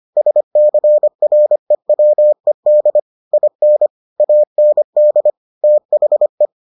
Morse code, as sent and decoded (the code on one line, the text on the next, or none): ... -.-. .-. .-- . -.. / .. -. / .- -. -.. / - .... .
SCREWED IN AND THE